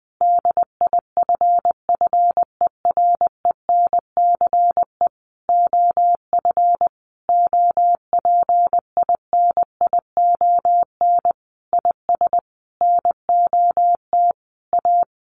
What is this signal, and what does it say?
Morse code, heard as a keyed tone, about 20 wpm, 700 Hz. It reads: DIFFERENCE OF OPINION IS NOT A
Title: Sentences from Top 1400 Words 20wpm